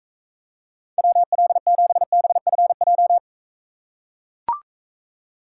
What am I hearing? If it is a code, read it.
Morse code, heard as a keyed tone, about 43 wpm, 700 Hz. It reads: WL7BFJ